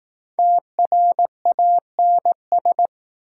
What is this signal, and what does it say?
Morse code, heard as a keyed tone, about 20 wpm, 700 Hz. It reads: TRANS